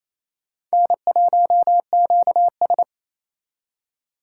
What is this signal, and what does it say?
Morse code, heard as a keyed tone, about 36 wpm, 700 Hz. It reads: N1QS